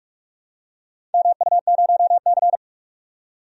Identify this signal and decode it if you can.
Morse code, heard as a keyed tone, about 46 wpm, 700 Hz. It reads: MU0C
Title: DXCC-100 Call Signs - 4 Characters - 45wpm